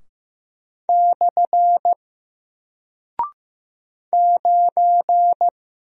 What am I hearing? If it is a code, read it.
Morse code, heard as a keyed tone, about 15 wpm, 700 Hz. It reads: / 9
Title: Single Character (Letter:Number:Punctuation:Prosign) Alt 15wpm